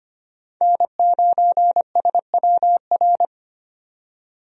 Morse code, heard as a keyed tone, about 25 wpm, 700 Hz. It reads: N9SWR
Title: USA Call Signs - 5 Characters - 25wpm